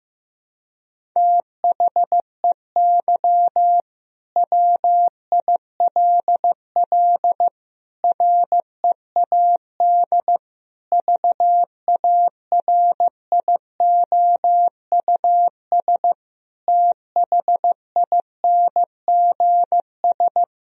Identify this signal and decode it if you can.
Morse code, heard as a keyed tone, about 15 wpm, 700 Hz. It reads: THEY WILL READ VARIOUS THINGS
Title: Sentences from Top 1000 Words - Review Part 4 - 15wpm